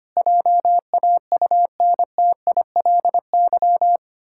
Morse code, heard as a keyed tone, about 25 wpm, 700 Hz. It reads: JAUNTILY
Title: Pangram Challenge v2 - 25wpm